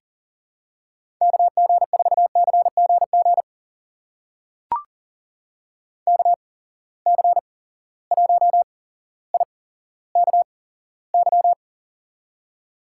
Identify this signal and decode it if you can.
Morse code, heard as a keyed tone, about 40 wpm, spaced as for 17 wpm, 700 Hz. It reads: KG4CGG KC1IKY
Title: USA Call Signs - 6 Characters - 8x Character Spacing - 40wpm